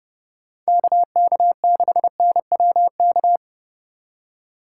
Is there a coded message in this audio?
Yes: Morse code from 0.7 to 3.4 s, heard as a keyed tone, about 29 wpm, 700 Hz. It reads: KK6NWK